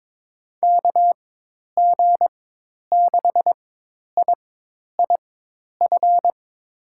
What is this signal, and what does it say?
Morse code, heard as a keyed tone, about 22 wpm, spaced as for 10 wpm, 700 Hz. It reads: KG6IIF
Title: USA Call Signs - 6 Characters - 4x Character Spacing - 22wpm